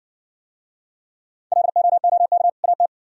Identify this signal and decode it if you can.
Morse code, heard as a keyed tone, about 57 wpm, 700 Hz. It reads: LOOK AT